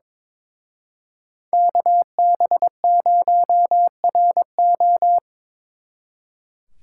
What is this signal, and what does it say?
Morse code, heard as a keyed tone, about 25 wpm, 700 Hz. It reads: KB0RO
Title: Speed Racing - USA 2x2 Call Signs 22wpm